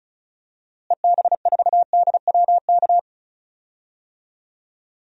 Morse code, heard as a keyed tone, about 34 wpm, 700 Hz. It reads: EB4DWK
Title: DXCC-100 Call Signs - 6 Characters - Repeated 3x - 35wpm